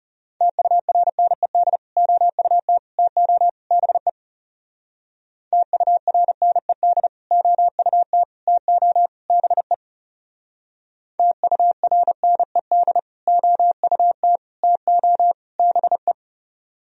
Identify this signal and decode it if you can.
Morse code, heard as a keyed tone, about 35 wpm, 700 Hz. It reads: TURNED OUT TO BE TURNED OUT TO BE TURNED OUT TO BE
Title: Speed-Racing - Sets of 4 Words 30wpm